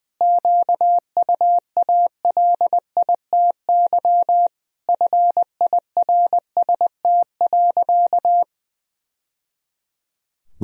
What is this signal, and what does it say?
Morse code, heard as a keyed tone, about 21 wpm, 700 Hz. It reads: QUALITY FIRST.
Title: Sentences from Top 1000 Words - Review Part 4 - 20wpm